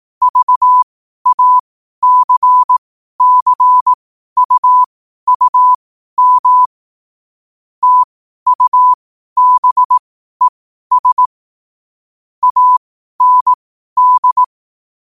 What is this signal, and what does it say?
Morse code, heard as a keyed tone, about 18 wpm, spaced as for 12 wpm, 1000 Hz. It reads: VACCUUM TUBES AND
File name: qotdfiles / 12wpm